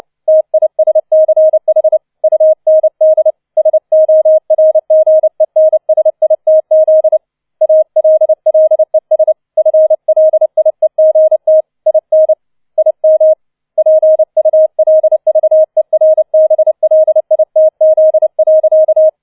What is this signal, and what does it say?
Morse code, heard as a keyed tone, about 29 wpm, 630 Hz. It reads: TISCH UND SORGENSITZ ALLES FLIEGT IN IM PULVERBLITZ.